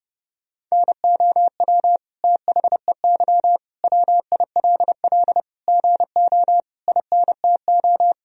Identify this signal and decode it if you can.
Morse code, heard as a keyed tone, about 30 wpm, 700 Hz. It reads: NOW THEY WILL GO INTO
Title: Sentences from Top 1000 Words - Review Part 4 - 30wpm